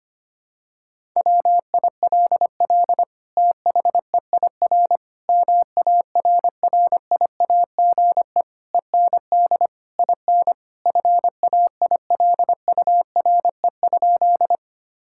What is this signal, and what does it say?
Morse code, heard as a keyed tone, about 25 wpm, 700 Hz. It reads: WILL THEIR MARRIAGE END IN FAILURE?
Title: Sentences from Top 1000 Words - Review Part 6 - 25wpm